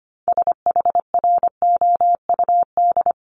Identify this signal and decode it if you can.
Morse code, heard as a keyed tone, about 25 wpm, 700 Hz. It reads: SHROUD
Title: Multi-letter Phonemes 25wpm